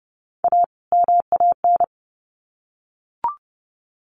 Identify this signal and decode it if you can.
Morse code, heard as a keyed tone, about 30 wpm, 700 Hz. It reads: A MAN
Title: Sets of 2 Words 30wpm